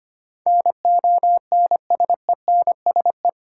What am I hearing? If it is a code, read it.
Morse code, heard as a keyed tone, about 25 wpm, 690 Hz. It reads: NONSENSE
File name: 8 Letter Words - First 3 Letters Given - 25wpm